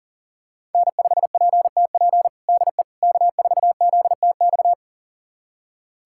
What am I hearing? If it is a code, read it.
Morse code, heard as a keyed tone, about 40 wpm, 700 Hz. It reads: N5PTP DE K4ZTX
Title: USA Callsign DE Callsign - 5 Characters - 40wpm